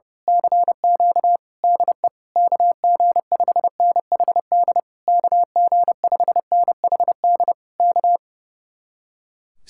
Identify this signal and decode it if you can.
Morse code, heard as a keyed tone, about 28 wpm, 700 Hz. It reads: CQ DE KG5NHD KG5NHD K
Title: Speed-Racing Common QSO Phrases - 30wpm